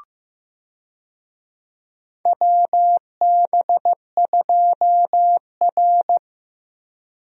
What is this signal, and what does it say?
Morse code, heard as a keyed tone, about 15 wpm, 700 Hz. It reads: WB2R
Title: USA Call Signs - 4 Characters - Repeated 2x - 15wpm